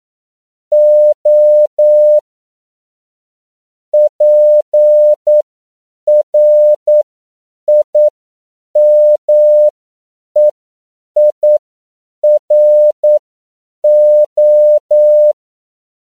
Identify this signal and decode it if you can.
Morse code, heard as a keyed tone, about 9 wpm, spaced as for 7 wpm, 600 Hz. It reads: O PRIMEIRO
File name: CW / 10PPM